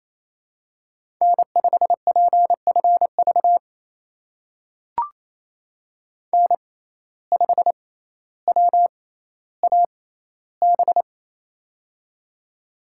Morse code, heard as a keyed tone, about 27 wpm, spaced as for 14 wpm, 700 Hz. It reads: N5PFV N5WAB
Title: USA Call Signs - 5 Characters - 6x Character Spacing - 28wpm